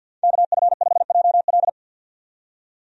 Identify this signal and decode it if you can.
Morse code, heard as a keyed tone, about 53 wpm, 700 Hz. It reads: KF5JL
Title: USA Call Signs - 5 Characters - 50wpm